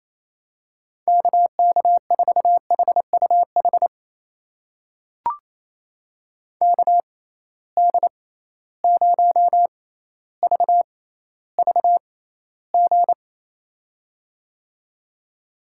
Morse code, heard as a keyed tone, about 28 wpm, spaced as for 14 wpm, 700 Hz. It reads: KK4HUH KD0VVG
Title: USA Call Signs - 6 Characters - 6x Character Spacing - 28wpm